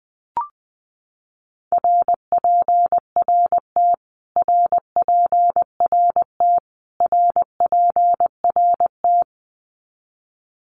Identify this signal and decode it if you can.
Morse code, heard as a keyed tone, about 20 wpm, 700 Hz. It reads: RPRT RPRT RPRT
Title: Abbreviations Repeated 3x - 20wpm